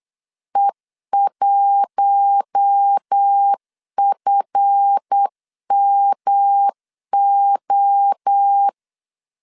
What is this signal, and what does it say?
Morse code, heard as a keyed tone, about 8 wpm, 800 Hz. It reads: E1FMO
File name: LEZIONI / 7 GF9Z / 7 GF9 8wpm 3545